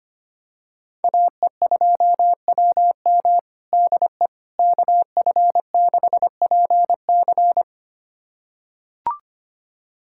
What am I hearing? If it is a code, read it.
Morse code, heard as a keyed tone, about 25 wpm, 700 Hz. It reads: AE2WM DE KF6PC
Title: USA Callsign DE Callsign - 5 Characters - 25wpm